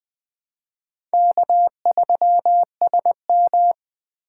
Morse code, heard as a keyed tone, about 20 wpm, 700 Hz. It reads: K3SM
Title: USA Call Signs - 4x Character Spacing - 20wpm